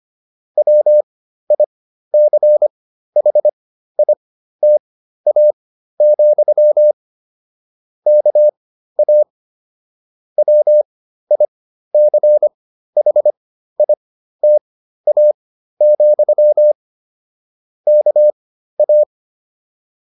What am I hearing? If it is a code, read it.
Morse code, heard as a keyed tone, about 25 wpm, spaced as for 13 wpm, 600 Hz. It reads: WICHITA, KA WICHITA, KA